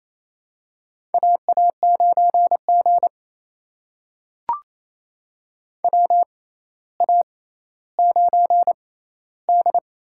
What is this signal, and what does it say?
Morse code, heard as a keyed tone, about 27 wpm, spaced as for 14 wpm, 700 Hz. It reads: AA9G WA9D
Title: USA Call Signs - 4 Characters - 6x Character Spacing - 28wpm